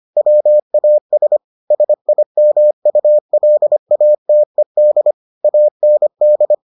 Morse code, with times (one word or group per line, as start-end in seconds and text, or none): 0.2-1.4 WAS
1.7-5.1 SIMULATED
5.4-6.5 AND